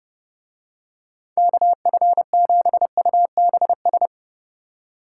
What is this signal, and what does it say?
Morse code, heard as a keyed tone, about 26 wpm, 700 Hz. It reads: KF7UBS